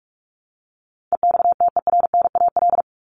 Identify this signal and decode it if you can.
Morse code, heard as a keyed tone, about 44 wpm, 700 Hz. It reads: EXTERNAL
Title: Top 1751-2000 Words 45wpm